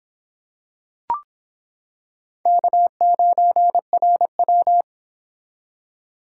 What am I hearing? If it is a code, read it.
Morse code, heard as a keyed tone, about 26 wpm, 700 Hz. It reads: K9RW